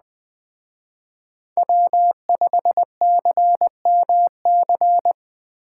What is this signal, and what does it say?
Morse code, heard as a keyed tone, about 21 wpm, 700 Hz. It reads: W5CMC
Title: Speed Racing - USA Call Signs 17wpm